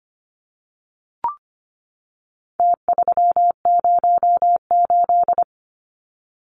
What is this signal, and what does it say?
Morse code, heard as a keyed tone, about 25 wpm, 700 Hz. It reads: T308
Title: Numbers - Contest Style - 4-Digit 25wpm